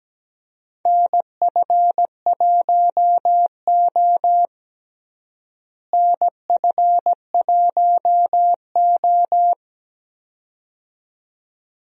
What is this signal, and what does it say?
Morse code, heard as a keyed tone, about 17 wpm, 700 Hz. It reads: NF1O NF1O
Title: USA Call Signs - 4 Characters - Repeated 2x - 17wpm